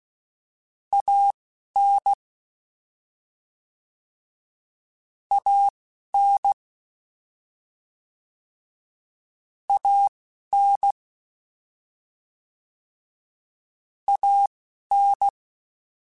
Morse code, heard as a keyed tone, about 16 wpm, spaced as for 11 wpm, 800 Hz. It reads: AN AN AN AN
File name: Short Runs / Short Runs-mp3s / ShortContrasts @10